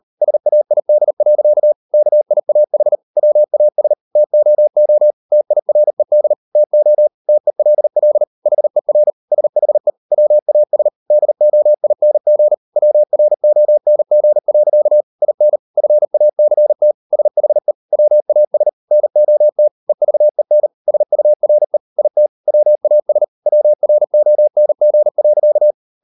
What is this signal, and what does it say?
Morse code, heard as a keyed tone, about 38 wpm, 600 Hz. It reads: SAID. KIAH WAS TOO TIRED TO TELL HER SHE WAS DOING WRONG. IN FACT SHE WAS NOT EVEN SURE IT WAS WRONG.